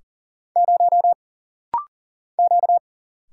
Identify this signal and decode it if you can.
Morse code, heard as a keyed tone, about 40 wpm, 700 Hz. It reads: 0 Q
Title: Single Letter-Number - Rapid-Fire 40wpm